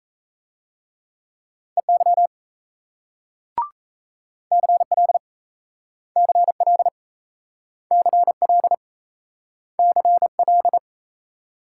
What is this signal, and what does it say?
Morse code, heard as a keyed tone, about 37 wpm, 700 Hz. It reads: EY CL CL CL CL